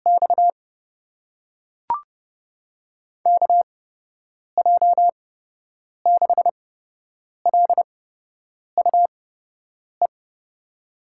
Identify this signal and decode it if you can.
Morse code, heard as a keyed tone, about 31 wpm, spaced as for 8 wpm, 700 Hz. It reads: X KJ6LUE
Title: USA Call Signs - 6 Characters - 8x Character Spacing - 30wpm